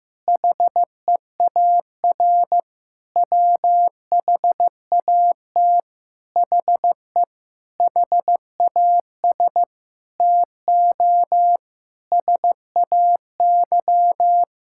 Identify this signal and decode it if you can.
Morse code, heard as a keyed tone, about 16 wpm, 700 Hz. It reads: HEAR WHAT HE HAS TO SAY
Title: Sentences from Top 1600 Words 15wpm